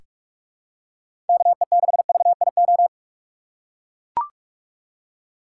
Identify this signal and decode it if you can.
Morse code, heard as a keyed tone, about 42 wpm, 700 Hz. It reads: KE6VIO
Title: USA Call Signs - 4x Character Spacing - 45wpm